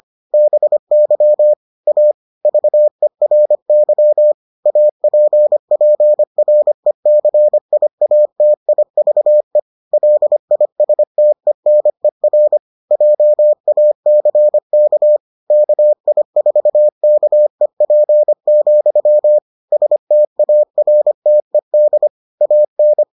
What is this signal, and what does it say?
Morse code, heard as a keyed tone, about 25 wpm, 600 Hz. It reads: BY A VERY APPRECIATIVE LISTENER JACK KI4KEP, STARTED AN